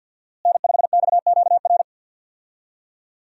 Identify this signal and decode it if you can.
Morse code, heard as a keyed tone, about 52 wpm, 700 Hz. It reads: N5XQR
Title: USA Call Signs - 5 Characters - Repeated 2x - 50wpm